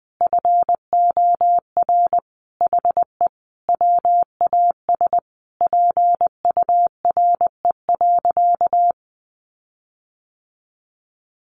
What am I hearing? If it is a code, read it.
Morse code, heard as a keyed tone, about 19 wpm, 700 Hz. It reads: FOR HE WAS PURE.
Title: Sentences from Top 1500 Words 20wpm